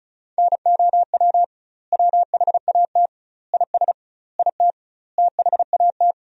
Morse code, heard as a keyed tone, about 36 wpm, 700 Hz. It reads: NOW WHAT IS IT THAT
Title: 2x Word-Spacing - Sentences from Top 200 Words - 35wpm